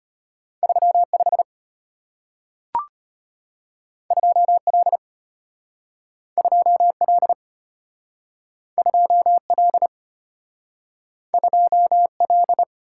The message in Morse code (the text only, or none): 35 2L 2L 2L 2L